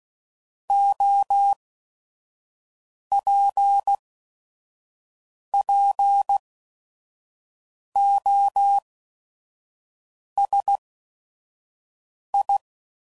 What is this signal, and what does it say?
Morse code, heard as a keyed tone, about 16 wpm, 800 Hz. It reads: O P P O S I